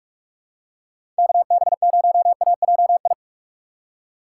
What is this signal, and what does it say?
Morse code, heard as a keyed tone, about 42 wpm, 700 Hz. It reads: KB0AJI
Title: USA Call Signs - 6 Characters - 4x Character Spacing - 45wpm